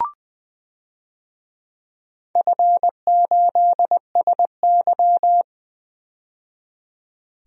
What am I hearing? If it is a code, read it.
Morse code, heard as a keyed tone, about 20 wpm, 700 Hz. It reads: F8SY
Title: DXCC-100 Call Signs - 4 Characters - 20wpm